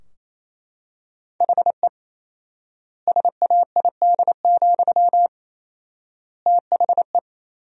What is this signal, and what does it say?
Morse code, heard as a keyed tone, about 28 wpm, 700 Hz. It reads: HE SAID, THE
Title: 4x Word-Spacing - Sentences from Top 400 Words - 28wpm